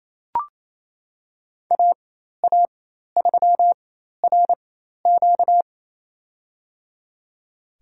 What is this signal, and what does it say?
Morse code, heard as a keyed tone, about 27 wpm, spaced as for 13 wpm, 700 Hz. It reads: AA3RQ
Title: USA Call Signs - 5 Characters - 4x Character Spacing - 28wpm